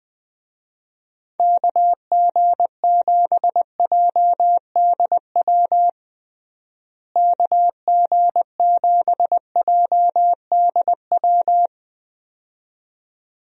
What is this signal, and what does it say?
Morse code, heard as a keyed tone, about 20 wpm, 700 Hz. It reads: KG7JDW KG7JDW
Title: USA Call Signs - 6 Characters - Repeated 2x - 20wpm